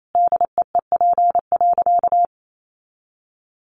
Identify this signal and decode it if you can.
Morse code, heard as a keyed tone, about 28 wpm, 700 Hz. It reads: DEEP.
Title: Sentences from Top 500 Words 28wpm